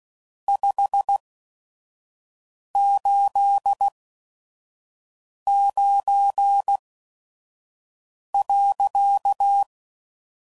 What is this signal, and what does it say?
Morse code, heard as a keyed tone, about 16 wpm, spaced as for 5 wpm, 800 Hz. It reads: 589.
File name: CW Code Groups @5WPM / QSOs / WD0FBT 2